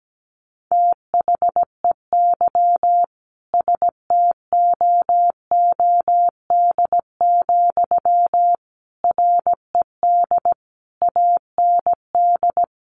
Morse code, heard as a keyed tone, about 17 wpm, 700 Hz. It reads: THEY STOOD, RED AND
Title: A Sound Of Thunder 17wpm